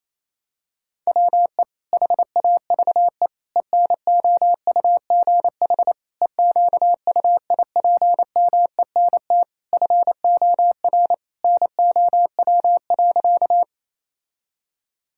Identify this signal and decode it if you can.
Morse code, heard as a keyed tone, about 27 wpm, 700 Hz. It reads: WE HAVE ENOUGH EQUIPMENT FOR NOW.
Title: Sentences from Top 1000 Words - Review Part 5 - 28wpm